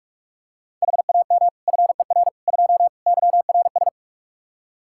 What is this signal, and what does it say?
Morse code, heard as a keyed tone, about 46 wpm, 700 Hz. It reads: HAM FER 2 YRS
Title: Practice QSO - Exchange 3 - More Info - 45wpm